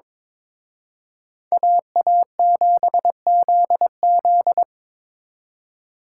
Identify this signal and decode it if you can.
Morse code, heard as a keyed tone, about 25 wpm, 700 Hz. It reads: AA7ZZ
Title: Speed Racing - USA 2x2 Call Signs 22wpm